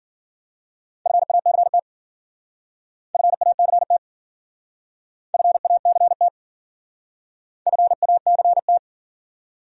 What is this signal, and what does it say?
Morse code, heard as a keyed tone, about 51 wpm, 700 Hz. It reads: FACT FACT FACT FACT